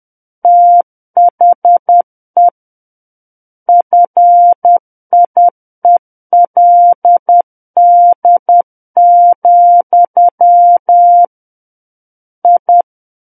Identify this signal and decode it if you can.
Morse code, heard as a keyed tone, about 10 wpm, 700 Hz. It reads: THE FIELD, I
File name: Book One / 01-Chapter 1 - 10 WPM